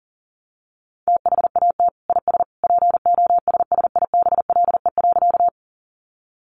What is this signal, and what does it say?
Morse code, heard as a keyed tone, about 40 wpm, 700 Hz. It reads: THAT IS POSSIBLE.
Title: Sentences from Top 500 Words 40wpm